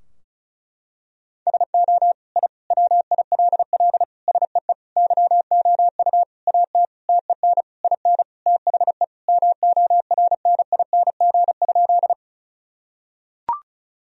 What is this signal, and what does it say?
Morse code, heard as a keyed tone, about 34 wpm, 700 Hz. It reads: SO I WILL SEE YOU AT TEN IN THE MORNING?
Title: Sentences from Top 1000 Words - Review Part 2 - 35wpm